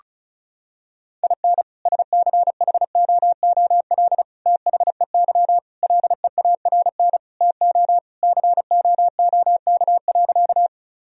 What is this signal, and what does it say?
Morse code, heard as a keyed tone, about 34 wpm, 700 Hz. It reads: IN SCHOOL THEY LEARN TO COOK.
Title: Sentences from Top 1000 Words - Review Part 4 - 35wpm